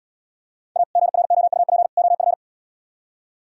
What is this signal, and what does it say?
Morse code, heard as a keyed tone, about 72 wpm, 700 Hz. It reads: A GROUP OF